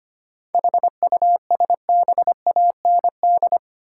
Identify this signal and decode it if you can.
Morse code, heard as a keyed tone, about 25 wpm, 700 Hz. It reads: HUSBAND
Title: Seven Letter Words - 25wpm